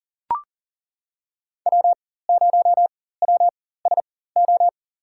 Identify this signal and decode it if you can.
Morse code, heard as a keyed tone, about 40 wpm, spaced as for 19 wpm, 700 Hz. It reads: W0WSO
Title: USA Call Signs - 5 Characters - 4x Character Spacing - 40wpm